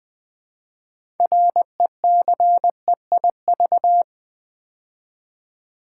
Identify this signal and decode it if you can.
Morse code, heard as a keyed tone, about 20 wpm, 700 Hz. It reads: RECEIV
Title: Build A Word - 2 Letters At A Time - 8-Letter Words - 20wpm